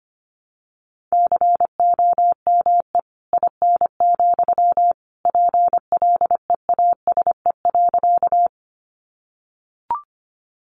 Morse code, heard as a keyed tone, about 25 wpm, 700 Hz. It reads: COME IN, PLEASE.
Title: Sentences from Top 1500 Words 25wpm